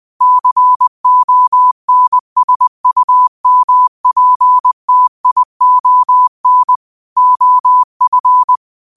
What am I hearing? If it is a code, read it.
Morse code, heard as a keyed tone, about 20 wpm, 1000 Hz. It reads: CONSUMPTION OF